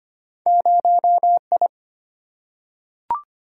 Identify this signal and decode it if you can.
Morse code, heard as a keyed tone, about 25 wpm, 700 Hz. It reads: 0I
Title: Single-Letter Single-Number 25wpm